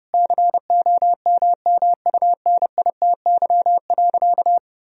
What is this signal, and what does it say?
Morse code, heard as a keyed tone, about 30 wpm, 700 Hz. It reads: COMMUNITY.
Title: Sentences from Top 1000 Words - Review Part 4 - 30wpm